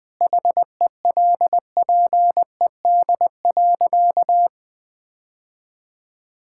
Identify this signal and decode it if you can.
Morse code, heard as a keyed tone, about 20 wpm, 700 Hz. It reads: HELPED.
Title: Sentences with 4-Word Phrases - 20wpm